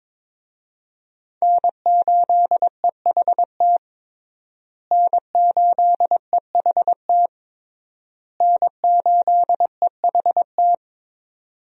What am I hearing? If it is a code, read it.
Morse code, heard as a keyed tone, about 22 wpm, 700 Hz. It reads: N8EHT N8EHT N8EHT
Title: USA Call Signs - 5 Characters - Repeated 3x - 22wpm